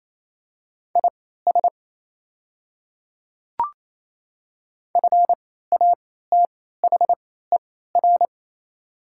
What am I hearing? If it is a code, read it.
Morse code, heard as a keyed tone, about 28 wpm, spaced as for 16 wpm, 700 Hz. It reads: IS FATHER
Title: 3x Character Spacing - Top 200 Words - 28wpm